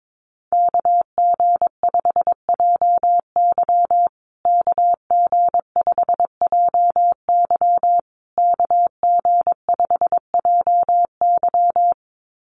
Morse code, heard as a keyed tone, about 22 wpm, 700 Hz. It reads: KG5JY KG5JY KG5JY
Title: USA Call Signs Repeated 3x 22wpm